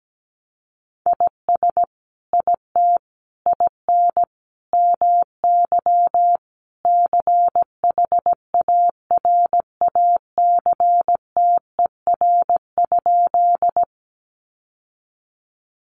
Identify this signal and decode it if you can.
Morse code, heard as a keyed tone, about 17 wpm, 700 Hz. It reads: IS IT IN MY CHARACTER?
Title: Sentences from Top 1200 Words 17wpm